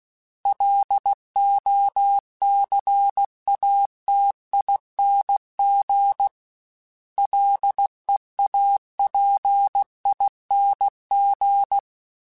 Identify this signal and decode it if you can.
Morse code, heard as a keyed tone, about 16 wpm, 800 Hz. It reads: LOCATING LEAPING
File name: Special INGs / Special INGs-mp3s / Sngl ing @15